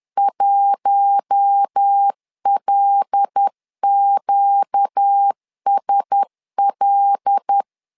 Morse code, heard as a keyed tone, about 10 wpm, 800 Hz. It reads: 1LQSL
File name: LEZIONI / 5 LBVC / 5 L 10wpm 2259